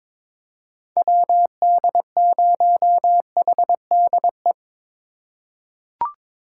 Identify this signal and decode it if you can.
Morse code, heard as a keyed tone, about 22 wpm, 700 Hz. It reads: WD0HDE
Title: USA Call Signs - 6 Characters - 4x Character Spacing - 22wpm